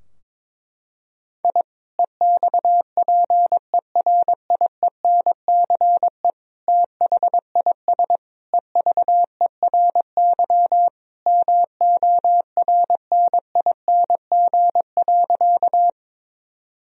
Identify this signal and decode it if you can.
Morse code, heard as a keyed tone, about 22 wpm, 700 Hz. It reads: I EXPERIENCE THIS EVERY MORNING.
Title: Sentences from Top 1000 Words - Review Part 3 - 22wpm